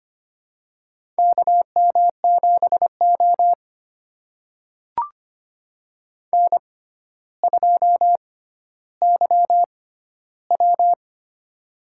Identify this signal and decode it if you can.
Morse code, heard as a keyed tone, about 25 wpm, spaced as for 13 wpm, 700 Hz. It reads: KM7O N2YW
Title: USA Call Signs - 4 Characters - 6x Character Spacing - 25wpm